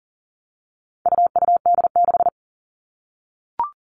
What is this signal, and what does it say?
Morse code, heard as a keyed tone, about 40 wpm, 700 Hz. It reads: UUD6
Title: All Cut Numbers - 4-Digit 40wpm